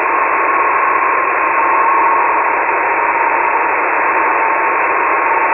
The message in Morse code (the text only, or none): none